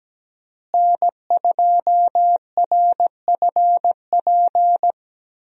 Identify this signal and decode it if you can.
Morse code, heard as a keyed tone, about 17 wpm, 700 Hz. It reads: N2RFP
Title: USA Call Signs - 5 Characters - Repeated 3x - 17wpm